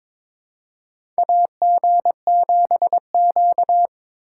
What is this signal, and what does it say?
Morse code, heard as a keyed tone, about 22 wpm, 700 Hz. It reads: AG7Q
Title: USA Call Signs - 3x Character Spacing - 22wpm